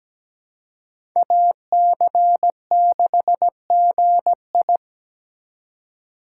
Morse code, heard as a keyed tone, about 17 wpm, 700 Hz. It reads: AC6GI
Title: USA Call Signs - 5 Characters - 4x Character Spacing - 17wpm